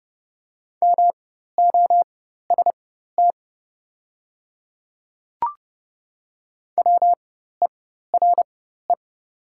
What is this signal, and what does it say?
Morse code, heard as a keyed tone, about 30 wpm, spaced as for 14 wpm, 700 Hz. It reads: MOST WERE
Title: 4x Character Spacing - Top 100 Words - 30wpm